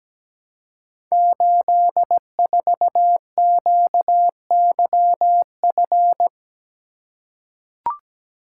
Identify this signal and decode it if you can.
Morse code, heard as a keyed tone, about 17 wpm, 700 Hz. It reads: 84QYF
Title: Code Groups - 5 Letter:Numbers - 17wpm